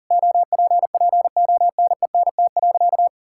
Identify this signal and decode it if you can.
Morse code, heard as a keyed tone, about 40 wpm, 700 Hz. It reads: OPPONENT.
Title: Pangram Challenge v2 - 40wpm